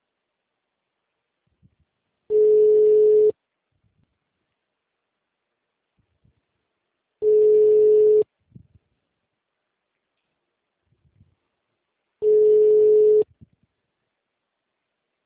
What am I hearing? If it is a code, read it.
Ringback tone (Europe, 425 Hz).